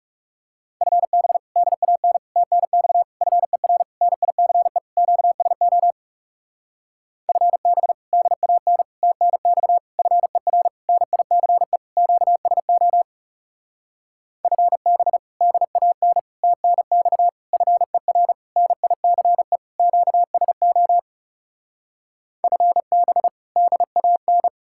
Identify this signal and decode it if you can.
Morse code, heard as a keyed tone, about 40 wpm, 700 Hz. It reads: FB DAN TNX FER NICE QSO FB DAN TNX FER NICE QSO FB DAN TNX FER NICE QSO FB DAN